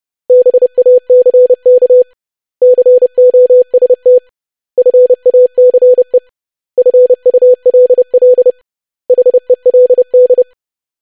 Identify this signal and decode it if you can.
Morse code, heard as a keyed tone, about 30 wpm, 500 Hz. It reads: BACK COST FACE FULL HELD
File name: lesson / web / basic500-25wpm